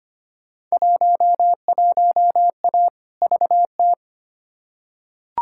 Morse code, heard as a keyed tone, about 25 wpm, 700 Hz. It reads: 11A VT
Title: Field Day Exchanges Encore - 25wpm